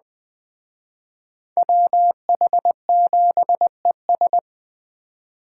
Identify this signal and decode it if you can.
Morse code, heard as a keyed tone, about 19 wpm, 700 Hz. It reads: WH7ES